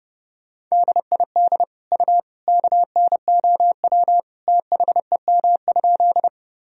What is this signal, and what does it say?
Morse code, heard as a keyed tone, about 30 wpm, 700 Hz. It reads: DID U KNOW THEM?
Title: Sentences from Top 100 Words and Abbreviations 30wpm